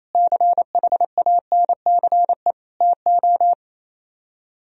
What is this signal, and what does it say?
Morse code, heard as a keyed tone, about 36 wpm, 700 Hz. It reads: CHANCE TO